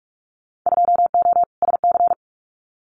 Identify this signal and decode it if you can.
Morse code, heard as a keyed tone, about 44 wpm, 700 Hz. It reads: 2O SC
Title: Winter Field Day Exchanges Encore 45wpm